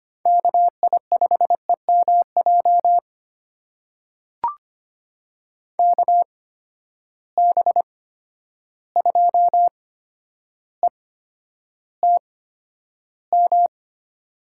Morse code, heard as a keyed tone, about 25 wpm, spaced as for 11 wpm, 700 Hz. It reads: KI5EMJ KB2ETM
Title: USA Call Signs - 6 Characters - 8x Character Spacing - 25wpm